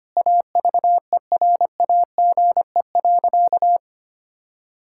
Morse code, heard as a keyed tone, about 25 wpm, 700 Hz. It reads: AVERAGE.
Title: Sentences from Top 1000 Words - Review Part 6 - 25wpm